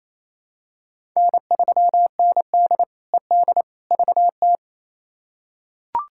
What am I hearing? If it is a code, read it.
Morse code, heard as a keyed tone, about 28 wpm, 700 Hz. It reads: N3ND ED VT
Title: K1USN SST QSO Practice - 28wpm